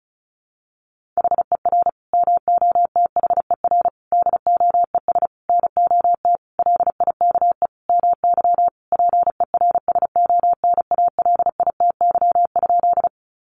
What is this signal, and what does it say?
Morse code, heard as a keyed tone, about 34 wpm, 700 Hz. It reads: HER MOTHER DOES NOT LIKE MY PERSONALITY?
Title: Sentences from Top 1400 Words 35wpm